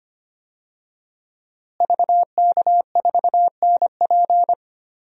Morse code, heard as a keyed tone, about 25 wpm, 700 Hz. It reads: VK4NP
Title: DXCC-100 Call Signs - 5 Characters - Repeated 3x - 25wpm